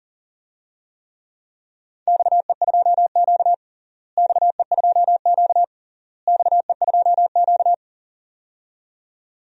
Morse code, heard as a keyed tone, about 40 wpm, 700 Hz. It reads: XE2Q XE2Q XE2Q
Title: DXCC-100 Call Signs - 4 Characters - Repeated 3x - 40wpm